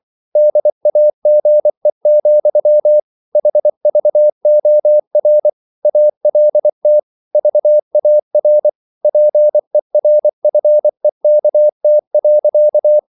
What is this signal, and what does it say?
Morse code, heard as a keyed tone, about 24 wpm, 600 Hz. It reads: DAGE, HVOR ALT VAR PERFEKT.
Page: Ferie 24 wpm